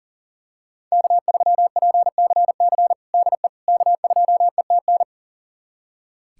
Morse code, heard as a keyed tone, about 40 wpm, 700 Hz. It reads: K3PCC DE K2ETN
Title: USA Callsign DE Callsign - 5 Characters - 40wpm